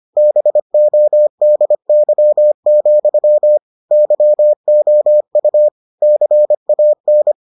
Morse code, heard as a keyed tone, about 25 wpm, 600 Hz. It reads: BODY, YOU CAN